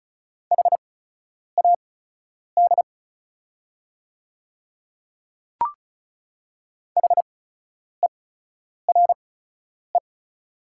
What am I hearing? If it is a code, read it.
Morse code, heard as a keyed tone, about 36 wpm, spaced as for 10 wpm, 690 Hz. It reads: HAD HERE
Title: 8x Character Spacing - Top 200 Words - 35wpm